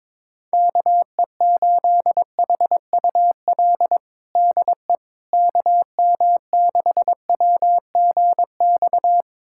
Morse code, heard as a keyed tone, about 22 wpm, 700 Hz. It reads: KE8HUL DE KM6WGX
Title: USA Callsign DE Callsign 22wpm